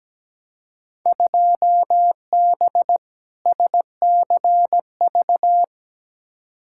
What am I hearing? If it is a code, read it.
Morse code, heard as a keyed tone, about 17 wpm, 700 Hz. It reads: 2B SCV
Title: Field Day Exchanges Encore - 17wpm